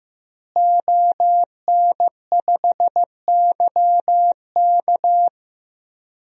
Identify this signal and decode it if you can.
Morse code, heard as a keyed tone, about 15 wpm, 700 Hz. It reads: ON5YK
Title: DXCC-100 Call Signs - 5 Characters - Repeated 2x - 15wpm